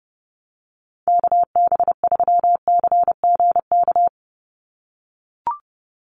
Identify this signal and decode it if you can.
Morse code, heard as a keyed tone, about 30 wpm, 700 Hz. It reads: KB3CGK
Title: USA Call Signs 30wpm